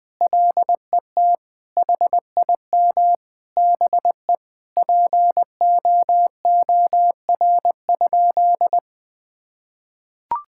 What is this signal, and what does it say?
Morse code, heard as a keyed tone, about 20 wpm, 700 Hz. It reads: LET HIM BE POOR?
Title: Sentences from Top 1600 Words 20wpm